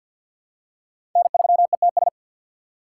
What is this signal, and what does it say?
Morse code, heard as a keyed tone, about 53 wpm, 700 Hz. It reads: N3ETS
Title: USA Call Signs - 5 Characters - 8x Character Spacing - 50wpm